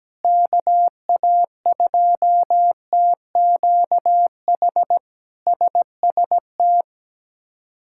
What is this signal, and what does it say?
Morse code, heard as a keyed tone, about 17 wpm, 700 Hz. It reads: KA2TQH SST
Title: K1USN SST QSO Practice - 17wpm